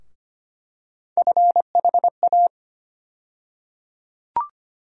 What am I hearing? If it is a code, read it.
Morse code, heard as a keyed tone, about 25 wpm, 700 Hz. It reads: FHA
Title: USA Call Sign Postfixes - 3 Letters 25wpm